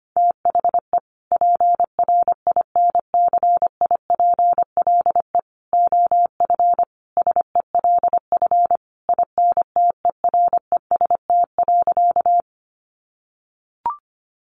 Morse code, heard as a keyed tone, about 25 wpm, 700 Hz. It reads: THE PRINCIPLE OF SELF INTEREST.
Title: Sentences from Top 1400 Words 25wpm